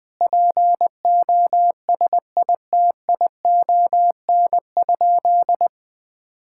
Morse code, heard as a keyed tone, about 20 wpm, 700 Hz. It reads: POSITION?
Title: Sentences from Top 1000 Words - Review Part 5 - 20wpm